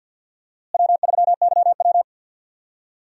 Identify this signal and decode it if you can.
Morse code, heard as a keyed tone, about 50 wpm, 700 Hz. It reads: W3YW
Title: USA Call Signs - 4 Characters - Repeated 3x - 50wpm